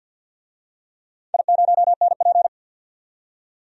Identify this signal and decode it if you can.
Morse code, heard as a keyed tone, about 48 wpm, 700 Hz. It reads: I0NP
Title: DXCC-100 Call Signs - 4 Characters - 50wpm